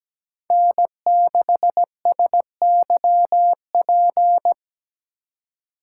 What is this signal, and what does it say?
Morse code, heard as a keyed tone, about 17 wpm, 700 Hz. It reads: N6SYP
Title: USA Call Signs - 5 Characters - 3x Character Spacing - 17wpm